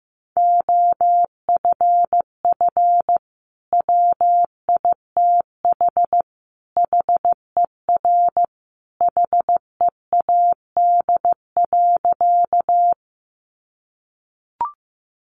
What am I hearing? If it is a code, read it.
Morse code, heard as a keyed tone, about 15 wpm, 700 Hz. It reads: OFF WITH HER HEAD.
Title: Sentences from Top 300 Words 15wpm